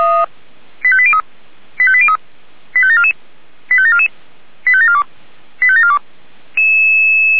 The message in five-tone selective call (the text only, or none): 85803 85803 85744 85744 85742 85742